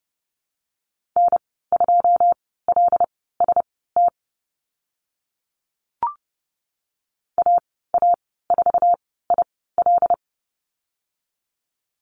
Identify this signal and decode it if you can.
Morse code, heard as a keyed tone, about 30 wpm, spaced as for 17 wpm, 700 Hz. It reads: N2LST AA4IL
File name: Lesson 14g - Number 2 - Call Signs 30wpm